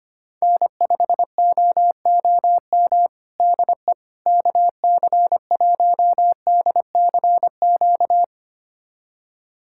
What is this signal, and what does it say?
Morse code, heard as a keyed tone, about 25 wpm, 700 Hz. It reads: N5OOM DE KC1DCQ
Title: USA Callsign DE Callsign 25wpm